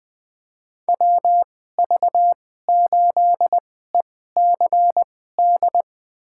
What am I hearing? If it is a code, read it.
Morse code, heard as a keyed tone, about 21 wpm, spaced as for 15 wpm, 700 Hz. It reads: WV8ECD
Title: USA Call Signs - 2x Character Spacing - 20wpm